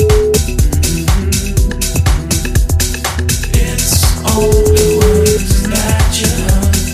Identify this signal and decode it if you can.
Ringback tone (Europe, 425 Hz), with noise.